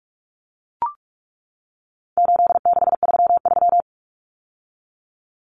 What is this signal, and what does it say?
Morse code, heard as a keyed tone, about 44 wpm, 700 Hz. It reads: 8633
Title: Numbers - 4-Digit 45wpm